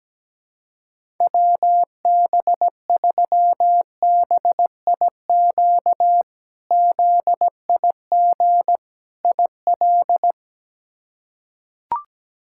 Morse code, heard as a keyed tone, about 17 wpm, 700 Hz. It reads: WB3BIQ ZIG IL
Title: K1USN SST QSO Practice - 17wpm